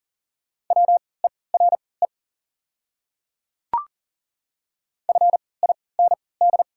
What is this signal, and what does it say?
Morse code, heard as a keyed tone, about 40 wpm, spaced as for 23 wpm, 690 Hz. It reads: WERE FIND
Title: 3x Character Spacing - Top 100 Words - 40wpm